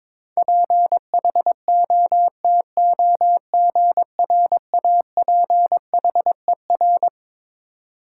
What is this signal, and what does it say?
Morse code, heard as a keyed tone, about 22 wpm, 700 Hz. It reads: PHOTOGRAPHER
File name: Twelve Letter Words - 22wpm